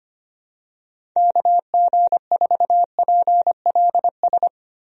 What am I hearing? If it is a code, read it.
Morse code, heard as a keyed tone, about 25 wpm, 700 Hz. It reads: KG4PLS
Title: USA Call Signs - 6 Characters - Repeated 3x - 25wpm